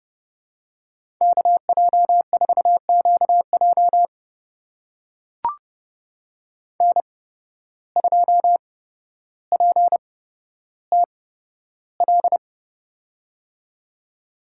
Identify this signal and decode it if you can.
Morse code, heard as a keyed tone, about 30 wpm, spaced as for 13 wpm, 700 Hz. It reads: KJ4QJ N2PTL
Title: USA Call Signs - 5 Characters - 8x Character Spacing - 30wpm